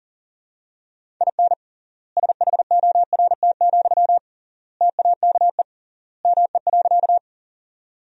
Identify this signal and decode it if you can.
Morse code, heard as a keyed tone, about 40 wpm, 700 Hz. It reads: IN SHORT, TAKE ME.
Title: 3x Word-Spacing - Sentences from Top 400 Words - 40wpm